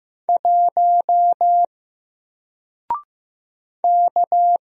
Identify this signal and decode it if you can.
Morse code, heard as a keyed tone, about 15 wpm, 700 Hz. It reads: 1 K
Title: Single Character (Letter:Number:Punctuation:Prosign) 15wpm